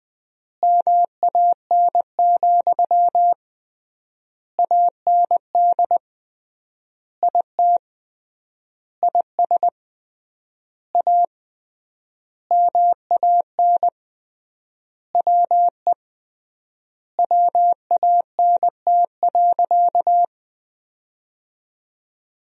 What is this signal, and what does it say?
Morse code, heard as a keyed tone, about 21 wpm, 700 Hz. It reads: MAN, AND IT IS A MAN WE WANT.
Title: 3x Word-Spacing - Sentences from Top 200 Words - 20wpm